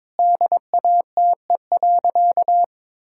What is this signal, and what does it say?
Morse code, heard as a keyed tone, about 22 wpm, 700 Hz. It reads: DATE.
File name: Sentences with 7-Word Phrases - 22wpm